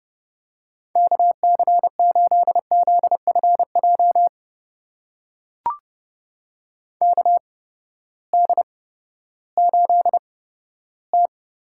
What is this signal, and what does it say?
Morse code, heard as a keyed tone, about 30 wpm, spaced as for 15 wpm, 700 Hz. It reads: KC8ZFJ KD8T
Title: USA Call Signs - 6 Characters - 8x Character Spacing - 30wpm